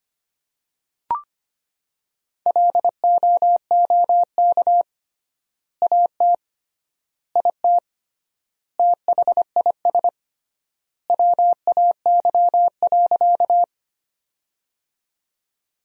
Morse code, heard as a keyed tone, about 25 wpm, 700 Hz. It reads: LOOK AT IT THIS WAY.
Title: 3x Word-Spacing - Sentences from Top 100 Words - 25wpm